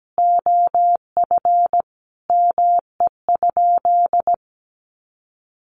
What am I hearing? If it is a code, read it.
Morse code, heard as a keyed tone, about 17 wpm, 700 Hz. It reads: OF ME?
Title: Sentences from Top 600 Words 17wpm